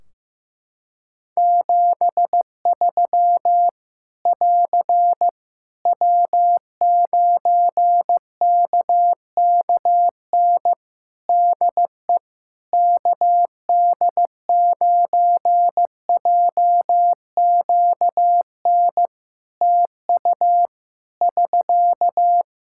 Morse code, heard as a keyed tone, about 15 wpm, 700 Hz. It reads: 73 <AR> W9KKN DE KD9JQN TU <SK>
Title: Practice QSO - Exchange 4 - Ending - 15wpm